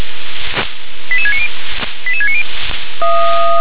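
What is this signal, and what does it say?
Five-tone selective call (ZVEI), heard as quick runs of five tones, tones of about 70 ms. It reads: 88690 88690
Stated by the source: ZVEI five-tone selective call received via SDR.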